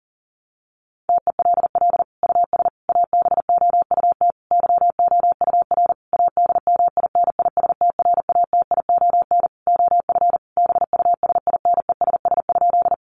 Morse code, heard as a keyed tone, about 40 wpm, 700 Hz. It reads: TELL US ABOUT YOUR ADMINISTRATION OF BUSINESS?
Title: Sentences from Top 1300 Words 40wpm